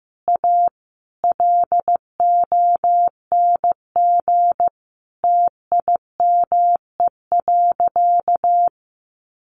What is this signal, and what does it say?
Morse code, heard as a keyed tone, about 15 wpm, 700 Hz. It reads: A LONG TIME.
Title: Sentences from Top 1000 Words 15wpm